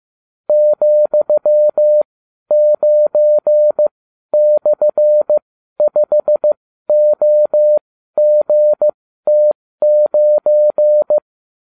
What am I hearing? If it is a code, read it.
Morse code, heard as a keyed tone, about 15 wpm, spaced as for 12 wpm, 600 Hz. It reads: ,9/5OGT9